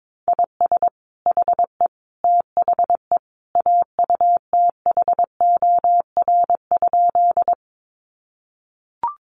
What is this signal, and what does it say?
Morse code, heard as a keyed tone, about 22 wpm, 700 Hz. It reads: IS HE THE AUTHOR?
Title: Sentences from Top 1200 Words 22wpm